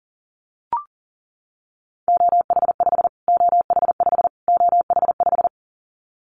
Morse code, heard as a keyed tone, about 40 wpm, 700 Hz. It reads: OH5 OH5 OH5
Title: Callsign Prefixes for Top 25 DXCC - Repeated 3x - 40wpm